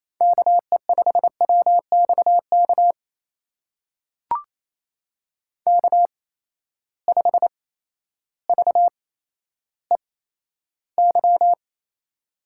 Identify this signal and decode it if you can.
Morse code, heard as a keyed tone, about 28 wpm, spaced as for 13 wpm, 700 Hz. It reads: KE5WXK K5VEY
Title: USA Call Signs - 8x Character Spacing - 28wpm